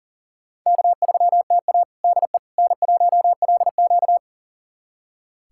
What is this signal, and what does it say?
Morse code, heard as a keyed tone, about 40 wpm, 700 Hz. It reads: K3TA DE N1LQ
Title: USA Callsign DE Callsign - 4 Characters - 40wpm